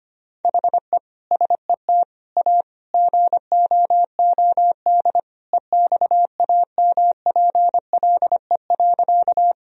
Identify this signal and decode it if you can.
Morse code, heard as a keyed tone, about 25 wpm, 700 Hz. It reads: HE SET A GOOD EXAMPLE.
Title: Sentences from Top 1000 Words - Review Part 1 - 25wpm